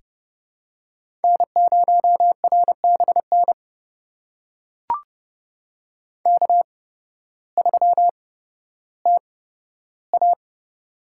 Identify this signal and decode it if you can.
Morse code, heard as a keyed tone, about 30 wpm, spaced as for 14 wpm, 700 Hz. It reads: N0RBN K3TA
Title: USA Call Signs - 8x Character Spacing - 30wpm